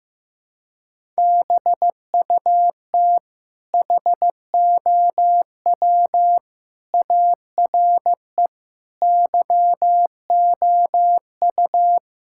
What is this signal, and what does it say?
Morse code, heard as a keyed tone, about 15 wpm, 700 Hz. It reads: BUT HOW ARE YOU
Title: Sentences from Top 1000 Words - Review Part 6 - 15wpm